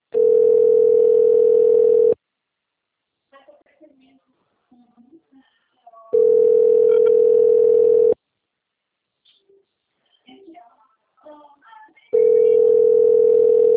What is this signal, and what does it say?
Ringback tone (North America), 440 + 480 Hz, with speech.
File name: rg-1600-unknown- - - .24899